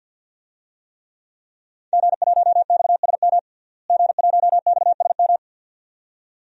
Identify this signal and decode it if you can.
Morse code, heard as a keyed tone, about 51 wpm, 700 Hz. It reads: G1XSM G1XSM
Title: DXCC-100 Call Signs - 5 Characters - Repeated 2x - 50wpm